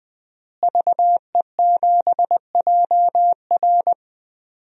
Morse code, heard as a keyed tone, about 20 wpm, 700 Hz. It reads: VE7JR
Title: DXCC-100 Call Signs - 5 Characters - Repeated 2x - 20wpm